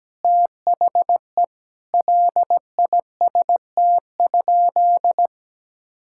Morse code, heard as a keyed tone, about 17 wpm, 700 Hz. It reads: THE LIST?
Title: Sentences from Top 1000 Words - Review Part 6 - 17wpm